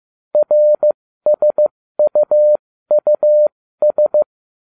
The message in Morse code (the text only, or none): RSUUS